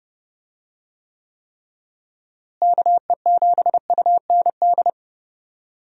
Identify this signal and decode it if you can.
Morse code, heard as a keyed tone, about 30 wpm, 700 Hz. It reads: KE7UND
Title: USA Call Signs - 6 Characters - Repeated 3x - 30wpm